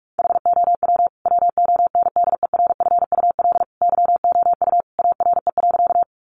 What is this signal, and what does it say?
Morse code, heard as a keyed tone, about 44 wpm, 700 Hz. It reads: HOW WONDERFUL YOU ARE.
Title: Sentences from Top 1100 Words 45wpm